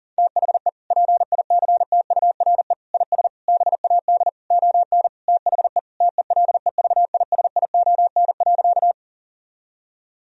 Morse code, heard as a keyed tone, about 40 wpm, 700 Hz. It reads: THE PICTURE IS BAD ON THE TELEVISION.
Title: Sentences from Top 1000 Words - Review Part 4 - 40wpm